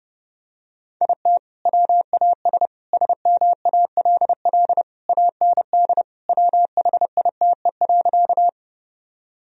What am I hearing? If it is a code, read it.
Morse code, heard as a keyed tone, about 32 wpm, 700 Hz. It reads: IT WAS SMALL AND WHITE.
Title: Sentences from Top 1000 Words - Review Part 1 - 30wpm